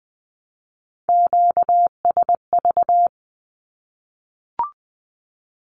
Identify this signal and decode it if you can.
Morse code, heard as a keyed tone, about 20 wpm, 700 Hz. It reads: QSV
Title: Q-Codes Repeated 3x 20wpm